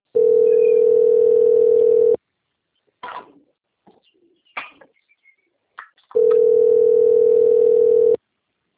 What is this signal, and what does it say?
Ringback tone (North America), 440 + 480 Hz.